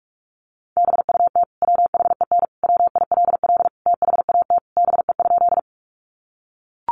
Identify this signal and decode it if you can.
Morse code, heard as a keyed tone, about 44 wpm, 700 Hz. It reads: BUT WHEN WILL THAT BE?
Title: Sentences from Top 100 Words 45wpm